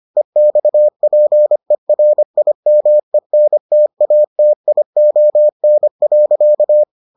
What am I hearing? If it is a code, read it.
Morse code, heard as a keyed tone, about 25 wpm, 600 Hz. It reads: EXPERIMENTATION.